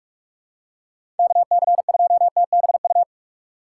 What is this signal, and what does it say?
Morse code, heard as a keyed tone, about 45 wpm, 700 Hz. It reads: KC2TBU
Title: USA Call Signs - 6 Characters - 4x Character Spacing - 45wpm